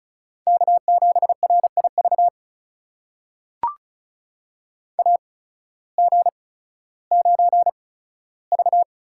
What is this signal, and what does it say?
Morse code, heard as a keyed tone, about 36 wpm, spaced as for 16 wpm, 700 Hz. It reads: K7RIV AG9V
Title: USA Call Signs - 8x Character Spacing - 35wpm